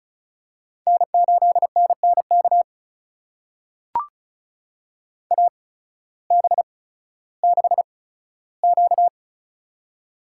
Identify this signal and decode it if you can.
Morse code, heard as a keyed tone, about 36 wpm, spaced as for 16 wpm, 700 Hz. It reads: N8NNK AB6Q
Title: USA Call Signs - 8x Character Spacing - 35wpm